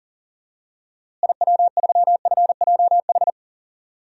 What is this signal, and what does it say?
Morse code, heard as a keyed tone, about 40 wpm, 700 Hz. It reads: IW3FJH
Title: DXCC-100 Call Signs - 6 Characters - Repeated 3x - 40wpm